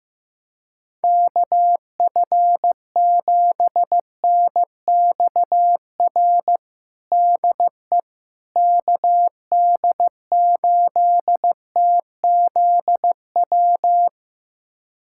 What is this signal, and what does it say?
Morse code, heard as a keyed tone, about 15 wpm, 700 Hz. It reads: KF7NXR DE KD8TZW
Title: USA Callsign DE Callsign - 6 Characters - 15wpm